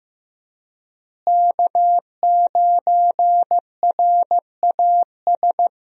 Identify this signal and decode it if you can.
Morse code, heard as a keyed tone, about 15 wpm, 700 Hz. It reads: K9RAS
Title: USA Call Signs - 5 Characters - Repeated 2x - 15wpm